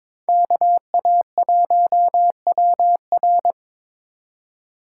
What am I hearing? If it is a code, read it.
Morse code, heard as a keyed tone, about 26 wpm, 700 Hz. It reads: KA1WR
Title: Speed Racing - USA 2x2 Call Signs 22wpm